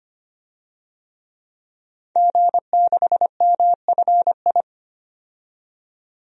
Morse code, heard as a keyed tone, about 25 wpm, 700 Hz. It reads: G6MFI
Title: DXCC-100 Call Signs - 5 Characters - 25wpm